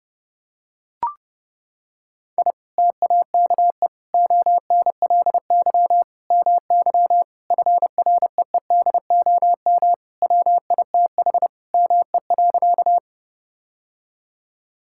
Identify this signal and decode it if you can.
Morse code, heard as a keyed tone, about 32 wpm, 700 Hz. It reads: I TAKE ONLY MY FREEDOM WITH ME.
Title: Sentences from Top 1000 Words - Review Part 5 - 30wpm